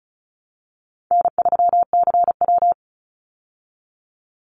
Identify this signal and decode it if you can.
Morse code, heard as a keyed tone, about 34 wpm, 700 Hz. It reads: N3CW
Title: Easy Call Signs Repeated 3x - 35wpm